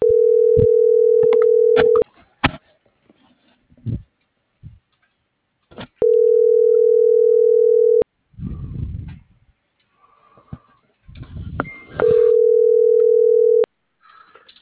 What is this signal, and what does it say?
Ringback tone (North America), 440 + 480 Hz.